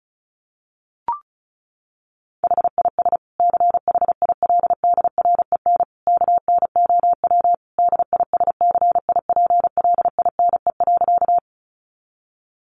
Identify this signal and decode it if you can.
Morse code, heard as a keyed tone, about 34 wpm, 700 Hz. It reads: HIS CHILDREN KNOW DISCIPLINE.
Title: Sentences from Top 900 Words 35wpm